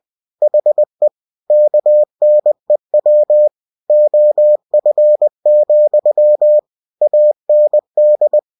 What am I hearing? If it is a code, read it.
Morse code, heard as a keyed tone, about 20 wpm, 600 Hz. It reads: HE KNEW OF, AND
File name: cw-books / winnie-the-pooh / winnie-20wpm-page-0006